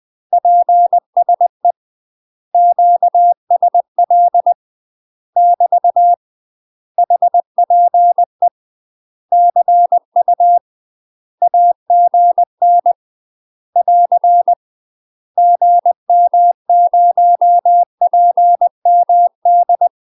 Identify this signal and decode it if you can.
Morse code, heard as a keyed tone, about 20 wpm, 700 Hz. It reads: PSE QSL <BT> HPE CU AGN <AR> GM0PMD
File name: UK MASTER QSOs TRAINING / 20WPM / UK MASTER QSOs 0022